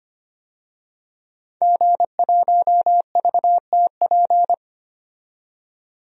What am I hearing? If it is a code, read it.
Morse code, heard as a keyed tone, about 25 wpm, 700 Hz. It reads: G1VTP
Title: DXCC-100 Call Signs - 5 Characters - 25wpm